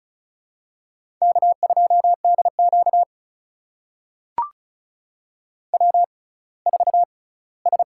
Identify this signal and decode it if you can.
Morse code, heard as a keyed tone, about 34 wpm, spaced as for 20 wpm, 700 Hz. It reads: K2DQ W4S
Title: USA Call Signs - 6x Character Spacing - 35wpm